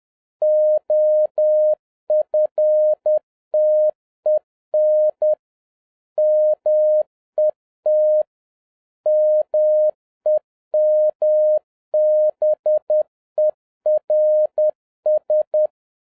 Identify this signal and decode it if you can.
Morse code, heard as a keyed tone, about 10 wpm, 610 Hz. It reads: OFTEN MET MEMBERS